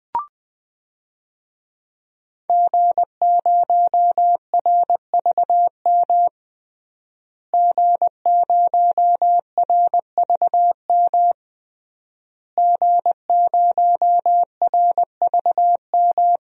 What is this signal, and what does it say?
Morse code, heard as a keyed tone, about 20 wpm, 700 Hz. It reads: G0RVM G0RVM G0RVM
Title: DXCC-100 Call Signs - 5 Characters - Repeated 3x - 20wpm